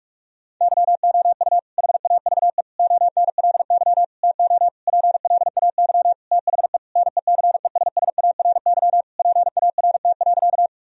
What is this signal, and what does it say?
Morse code, heard as a keyed tone, about 46 wpm, 700 Hz. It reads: YOU HAVE ONLY TO PLAY THE NECESSARY PART.
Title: Sentences from Top 1000 Words - Review Part 1 - 45wpm